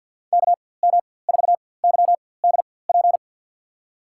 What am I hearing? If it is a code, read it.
Morse code, heard as a keyed tone, about 52 wpm, spaced as for 24 wpm, 700 Hz. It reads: KM4YDP
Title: USA Call Signs - 6 Characters - 4x Character Spacing - 50wpm